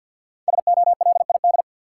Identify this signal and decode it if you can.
Morse code, heard as a keyed tone, about 52 wpm, 700 Hz. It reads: SOLID
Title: Sentences from Top 1500 Words - New Word Drill - 50wpm